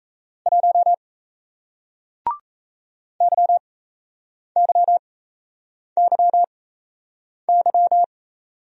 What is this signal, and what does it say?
Morse code, heard as a keyed tone, about 38 wpm, spaced as for 8 wpm, 700 Hz. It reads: 1 YYYY